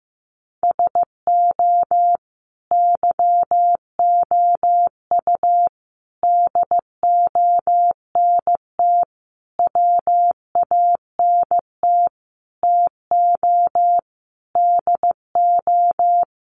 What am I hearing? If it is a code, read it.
Morse code, heard as a keyed tone, about 15 wpm, 700 Hz. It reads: SO YOU DONT WANT TO DO
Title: Sentences from Top 1000 Words 15wpm